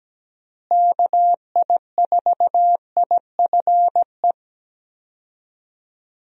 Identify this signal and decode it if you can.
Morse code, heard as a keyed tone, about 17 wpm, 700 Hz. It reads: KI4IFE
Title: USA Call Signs - 6 Characters - Repeated 3x - 17wpm